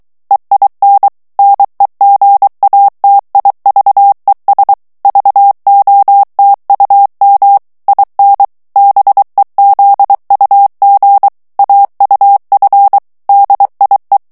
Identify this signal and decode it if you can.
Morse code, heard as a keyed tone, about 23 wpm, 800 Hz. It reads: EIN NEGATIVES VOTUM IN BEZUG AUF DIE
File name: morse mp3 / ALLEMAND texte clair 1 vitesse 22 mots minute